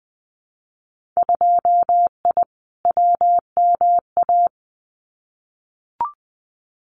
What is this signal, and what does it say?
Morse code, heard as a keyed tone, about 20 wpm, 700 Hz. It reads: 2I WMA
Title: Winter Field Day Exchanges Encore 20wpm